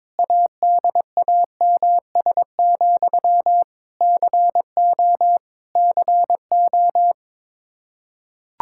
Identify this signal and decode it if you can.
Morse code, heard as a keyed tone, about 22 wpm, 700 Hz. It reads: ADAMS, CO CO
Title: Practice QSO - Exchange 1 - Beginning - 22wpm